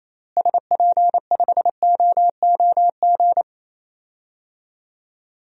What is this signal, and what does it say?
Morse code, heard as a keyed tone, about 28 wpm, 700 Hz. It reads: SP5OOG
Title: DXCC-100 Call Signs - 6 Characters - Repeated 3x - 28wpm